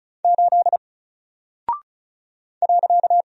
Morse code, heard as a keyed tone, about 36 wpm, 700 Hz. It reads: 8 .
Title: Single Character (Letter:Number:Punctuation:Prosign) 35wpm